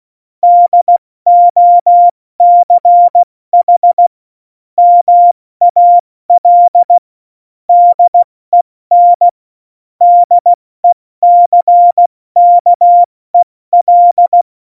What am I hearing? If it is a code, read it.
Morse code, heard as a keyed tone, about 16 wpm, 700 Hz. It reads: DOCH MAL DEN DECKEL
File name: Morsekurs / Witze / Witz 14. November 2017, 70 F80